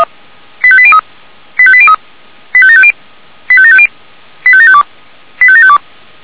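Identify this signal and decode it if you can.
Five-tone selective call (ZVEI), heard as quick runs of five tones, tones of about 70 ms. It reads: 85803 85803 85750 85750 85752 85752
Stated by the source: Five-tone ZVEI selective call received on an SDR.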